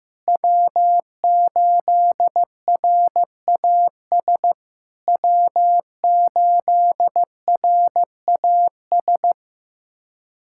Morse code, heard as a keyed tone, about 15 wpm, 700 Hz. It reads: W8RAS W8RAS
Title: SOTA QSO Practice 15wpm